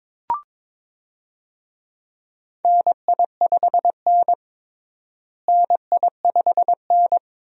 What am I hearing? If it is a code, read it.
Morse code, heard as a keyed tone, about 22 wpm, 700 Hz. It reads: NI5N NI5N
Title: USA Call Signs - 4 Characters - Repeated 2x - 22wpm